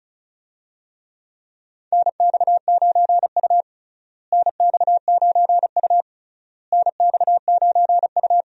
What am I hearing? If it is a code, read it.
Morse code, heard as a keyed tone, about 34 wpm, 700 Hz. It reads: NX9U NX9U NX9U
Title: USA Call Signs - 4 Characters - Repeated 3x - 35wpm